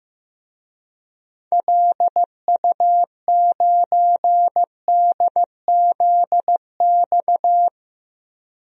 Morse code, heard as a keyed tone, about 15 wpm, 700 Hz. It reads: LU9DZX
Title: DXCC-100 Call Signs - 6 Characters - 15wpm